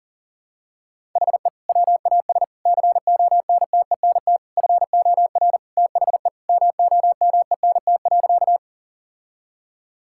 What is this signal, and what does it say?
Morse code, heard as a keyed tone, about 40 wpm, 700 Hz. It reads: HE WAS CONTENT FOR THE MOMENT.
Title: Sentences from Top 1000 Words - Review Part 6 - 40wpm